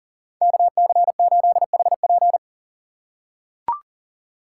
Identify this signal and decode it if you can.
Morse code, heard as a keyed tone, about 40 wpm, 700 Hz. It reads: KC8HP
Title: USA Call Signs - 4x Character Spacing - 40wpm